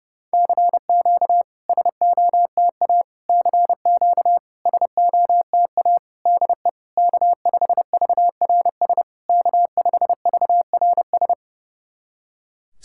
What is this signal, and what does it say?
Morse code, heard as a keyed tone, about 30 wpm, 690 Hz. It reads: CQ SOTA CQ SOTA DE K5VRS K5VRS
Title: SOTA QSO Practice 30wpm